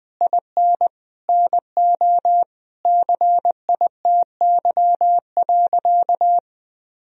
Morse code, heard as a keyed tone, about 20 wpm, 700 Hz. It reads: IN NO CITY.
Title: Sentences from Top 1000 Words - Review Part 1 - 20wpm